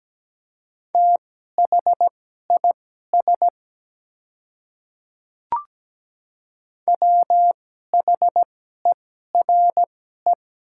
Morse code, heard as a keyed tone, about 17 wpm, spaced as for 12 wpm, 700 Hz. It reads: THIS WHERE
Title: 2x Character Spacing - Top 300 Words - 17wpm